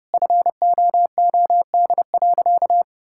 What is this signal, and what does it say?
Morse code, heard as a keyed tone, about 32 wpm, 700 Hz. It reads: FOOD.
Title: Sentences from Top 1000 Words - Review Part 5 - 30wpm